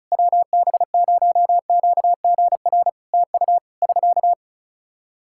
Morse code, heard as a keyed tone, about 36 wpm, 700 Hz. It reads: WB0QGR TU <SK>
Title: Practice QSO - Exchange 4 - Ending - 35wpm